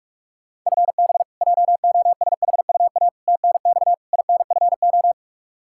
Morse code, heard as a keyed tone, about 46 wpm, 700 Hz. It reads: FB JOSHUA TNX INFO
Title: K1USN SST QSO Practice 45wpm